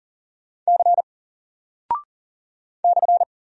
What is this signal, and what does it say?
Morse code, heard as a keyed tone, about 39 wpm, 700 Hz. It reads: C /
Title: Single Character (Letter:Number:Punctuation:Prosign) 40wpm